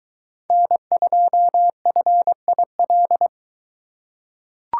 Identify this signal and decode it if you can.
Morse code, heard as a keyed tone, about 22 wpm, 700 Hz. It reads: N2FIL